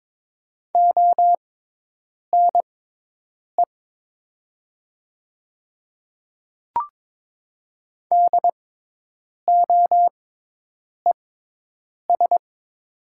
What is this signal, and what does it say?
Morse code, heard as a keyed tone, about 22 wpm, spaced as for 8 wpm, 700 Hz. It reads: ONE DOES
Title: 6x Character Spacing - Top 300 Words - 22wpm